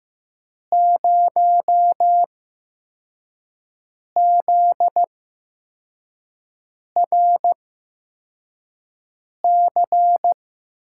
Morse code, heard as a keyed tone, about 15 wpm, spaced as for 4 wpm, 700 Hz. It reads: 0ZRC
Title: USA Call Signs - 6 Characters - 8x Character Spacing - 15wpm